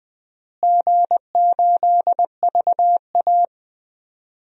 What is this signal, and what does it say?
Morse code, heard as a keyed tone, about 20 wpm, 700 Hz. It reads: G8VA
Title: DXCC-100 Call Signs - 4 Characters - Repeated 2x - 20wpm